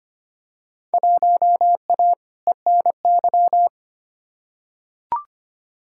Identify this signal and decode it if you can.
Morse code, heard as a keyed tone, about 25 wpm, 700 Hz. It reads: 1A ENY
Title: Field Day Exchanges - 25wpm